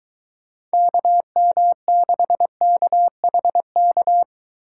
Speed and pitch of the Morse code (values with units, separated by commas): 22 wpm, 700 Hz